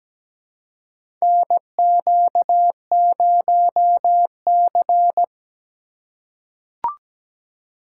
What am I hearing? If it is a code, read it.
Morse code, heard as a keyed tone, about 17 wpm, 700 Hz. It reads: NQ0C
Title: USA Call Signs - 4 Characters - 8x Character Spacing - 17wpm